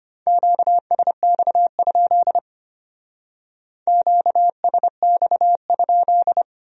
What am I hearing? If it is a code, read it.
Morse code, heard as a keyed tone, about 30 wpm, 700 Hz. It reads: QSX? QSX?
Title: Speed-Racing Q-Codes - 25wpm